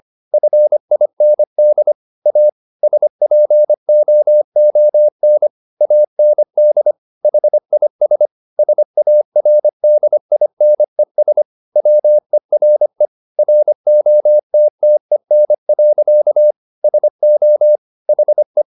Morse code, heard as a keyed tone, about 25 wpm, 600 Hz. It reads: FIND A SPOON AND HIS SARDINES WERE ROTTEN. SO HE